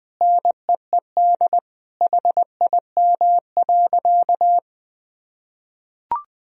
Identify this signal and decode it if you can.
Morse code, heard as a keyed tone, about 20 wpm, 700 Hz. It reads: NEED HIM.
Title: Sentences from Top 1000 Words - Review Part 1 - 20wpm